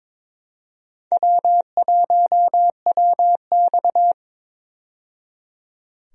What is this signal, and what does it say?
Morse code, heard as a keyed tone, about 22 wpm, 700 Hz. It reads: W1WX
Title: USA Call Signs - 4 Characters - 22wpm